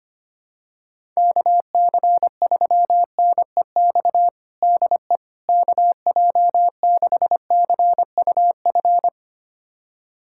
Morse code, heard as a keyed tone, about 25 wpm, 700 Hz. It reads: KC3NEX DE KJ6CUF
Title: USA Callsign DE Callsign - 6 Characters - 25wpm